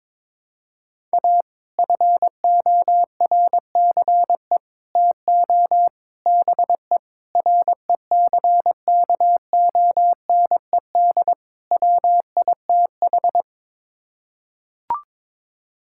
Morse code, heard as a keyed tone, about 22 wpm, 700 Hz. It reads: A FORCE TO BE RECKONED WITH
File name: practice / 22wpm / Sets of 6 Words - 22wpm